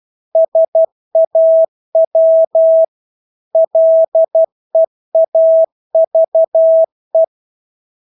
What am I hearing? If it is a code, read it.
Morse code, heard as a keyed tone, about 12 wpm, 650 Hz. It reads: SAW LEAVE
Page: Engelske ord 12 wpm